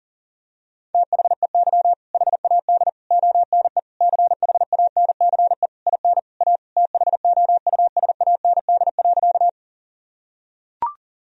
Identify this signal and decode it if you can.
Morse code, heard as a keyed tone, about 40 wpm, 700 Hz. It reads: THEY HAD ONE CHANCE IN A THOUSAND.
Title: Sentences from Top 1000 Words - Review Part 3 - 40wpm